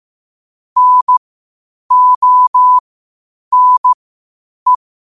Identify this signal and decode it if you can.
Morse code, heard as a keyed tone, about 15 wpm, spaced as for 8 wpm, 1000 Hz. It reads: NONE
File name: qotdfiles / 8wpm